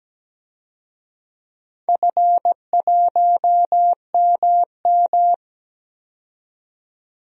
Morse code, heard as a keyed tone, about 17 wpm, 700 Hz. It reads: F1MM
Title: DXCC-100 Call Signs - 4 Characters - 17wpm